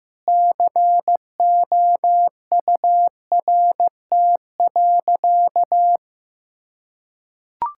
Morse code, heard as a keyed tone, about 15 wpm, 700 Hz. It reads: COURT.
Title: Sentences from Top 1600 Words 15wpm